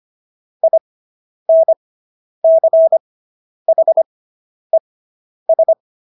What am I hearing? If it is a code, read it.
Morse code, heard as a keyed tone, about 25 wpm, spaced as for 10 wpm, 650 Hz. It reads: INCHES